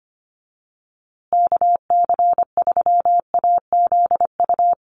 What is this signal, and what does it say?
Morse code, heard as a keyed tone, about 25 wpm, 700 Hz. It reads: KC3AZU
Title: USA Call Signs Repeated 2x 25wpm